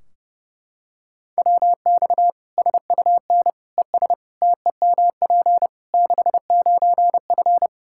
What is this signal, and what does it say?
Morse code, heard as a keyed tone, about 30 wpm, 700 Hz. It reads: WX SUN ES TEMP 69F
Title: K1USN SST QSO Practice 30wpm